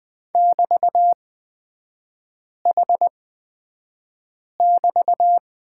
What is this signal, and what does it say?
Morse code, heard as a keyed tone, about 20 wpm, spaced as for 5 wpm, 700 Hz. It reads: <BT>H<BT>
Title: 224 Lesson 36e - BT - Three Characters 20wpm